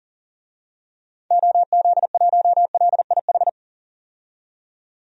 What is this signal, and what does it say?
Morse code, heard as a keyed tone, about 40 wpm, 700 Hz. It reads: OZ1LIH
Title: DXCC-100 Call Signs - 6 Characters - 40wpm